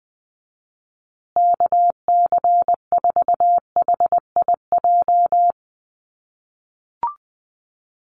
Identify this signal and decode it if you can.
Morse code, heard as a keyed tone, about 20 wpm, 700 Hz. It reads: KC4HIJ
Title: USA Call Signs Repeated 2x 20wpm